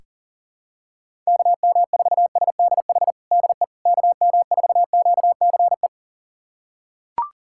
Morse code, heard as a keyed tone, about 40 wpm, 700 Hz. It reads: KM4SDH DE KM4QCE
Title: USA Callsign DE Callsign 40wpm